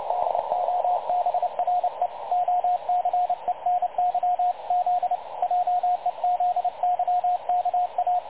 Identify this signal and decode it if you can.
Morse code, heard as a keyed tone, about 30 wpm, 720 Hz, with radio noise. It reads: DOBRE OCENY Z JEZYKA
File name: dlazaawansowanych / nagraniazpasma / websdr recording 3565.0kHz